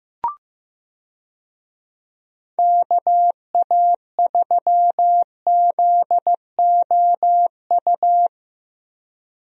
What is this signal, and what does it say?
Morse code, heard as a keyed tone, about 15 wpm, 700 Hz. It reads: KA3ZOU
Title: USA Call Signs - 6 Characters - 15wpm